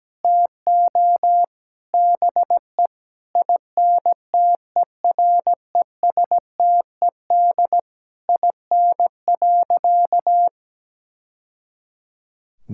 Morse code, heard as a keyed tone, about 17 wpm, 700 Hz. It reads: TO BE INTERESTED IN.
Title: Sentences from Top 1000 Words - Review Part 6 - 17wpm